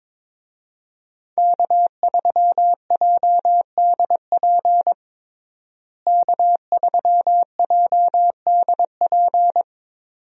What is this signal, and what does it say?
Morse code, heard as a keyed tone, about 22 wpm, 700 Hz. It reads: K3JDP K3JDP
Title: USA Call Signs - 5 Characters - Repeated 2x - 22wpm